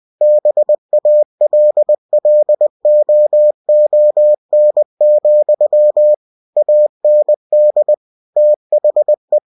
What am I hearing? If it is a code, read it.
Morse code, heard as a keyed tone, about 20 wpm, 600 Hz. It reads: BALLOON, AND THE